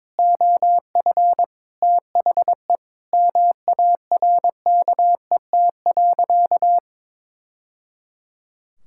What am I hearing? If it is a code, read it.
Morse code, heard as a keyed tone, about 21 wpm, 700 Hz. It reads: OF THE MARKET.
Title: Sentences from Top 1000 Words - Review Part 6 - 22wpm